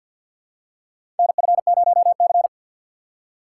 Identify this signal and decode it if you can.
Morse code, heard as a keyed tone, about 53 wpm, 700 Hz. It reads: NF0C
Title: USA Call Signs - 4 Characters - 6x Character Spacing - 50wpm